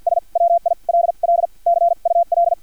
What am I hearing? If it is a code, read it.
Morse code, heard as a keyed tone, about 50 wpm, spaced as for 41 wpm, 690 Hz. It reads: S WILL CUL
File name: archived / CRSnet